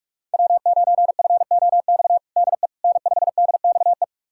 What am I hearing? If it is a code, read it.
Morse code, heard as a keyed tone, about 41 wpm, 700 Hz. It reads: W9FOX DE N5DXE
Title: USA Callsign DE Callsign - 5 Characters - 45wpm